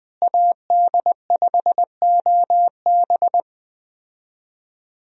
Morse code, heard as a keyed tone, about 20 wpm, 700 Hz. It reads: AD5OB
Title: USA Call Signs - 5 Characters - Repeated 2x - 20wpm